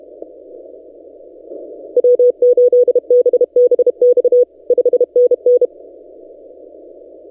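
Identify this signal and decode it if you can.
Morse code, heard as a keyed tone, about 30 wpm, 490 Hz. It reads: W8BBX 5NN
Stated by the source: Morse from DXpedition station A35X on 12 m, posted by N6ML.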